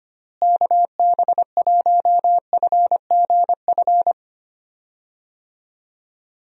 Morse code, heard as a keyed tone, about 25 wpm, 700 Hz. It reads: KB1FGF
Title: USA Call Signs - 6 Characters - Repeated 2x - 25wpm